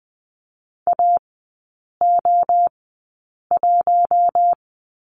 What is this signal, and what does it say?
Morse code, heard as a keyed tone, about 20 wpm, spaced as for 8 wpm, 700 Hz. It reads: AO1
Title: 041 Lesson 8e - Number 1 - Three Characters 20wpm